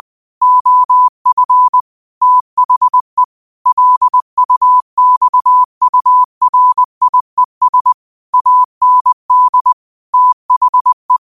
Morse code, heard as a keyed tone, about 20 wpm, 1000 Hz. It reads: OF THE LUXURIES AND THE
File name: qotdfiles / 20wpm